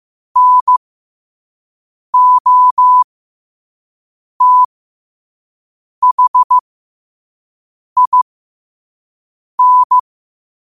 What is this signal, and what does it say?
Morse code, heard as a keyed tone, about 15 wpm, spaced as for 5 wpm, 1000 Hz. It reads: NOTHIN
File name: qotdfiles / 5wpm